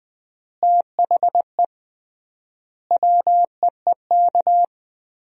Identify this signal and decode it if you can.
Morse code, heard as a keyed tone, about 20 wpm, 700 Hz. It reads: THE WEEK
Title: 3x Word-Spacing - Sentences from Top 500 Words - 20wpm